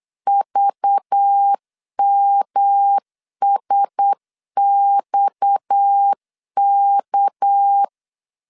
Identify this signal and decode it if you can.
Morse code, heard as a keyed tone, about 8 wpm, 800 Hz. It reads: VMSXK
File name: LEZIONI / 6 PKX0 / 6 PKX0 8wpm 3341